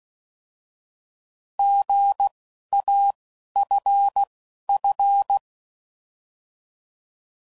Morse code, heard as a keyed tone, about 16 wpm, spaced as for 11 wpm, 800 Hz. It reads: GAFF
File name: Prefixes / By First Letter